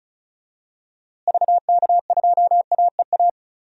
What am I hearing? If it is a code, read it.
Morse code, heard as a keyed tone, about 34 wpm, 700 Hz. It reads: VK2AEA
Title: DXCC-100 Call Signs - 6 Characters - Repeated 2x - 35wpm